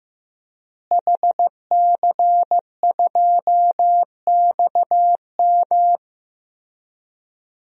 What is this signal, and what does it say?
Morse code, heard as a keyed tone, about 16 wpm, 700 Hz. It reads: HC2XM
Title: DXCC-100 Call Signs - 5 Characters - Repeated 2x - 15wpm